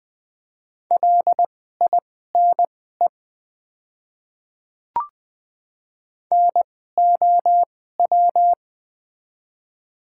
Morse code, heard as a keyed tone, about 20 wpm, spaced as for 14 wpm, 700 Hz. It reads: LINE NOW
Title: 2x Character Spacing - Top 200 Words - 20wpm